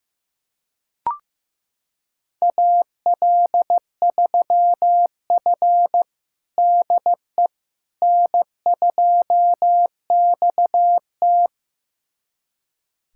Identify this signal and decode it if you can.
Morse code, heard as a keyed tone, about 16 wpm, 700 Hz. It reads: AL3F DE N2XT
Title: USA Callsign DE Callsign - 4 Characters - 15wpm